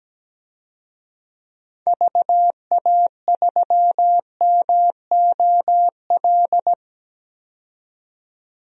Morse code, heard as a keyed tone, about 17 wpm, 700 Hz. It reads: VA3MOL
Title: DXCC-100 Call Signs - 6 Characters - 17wpm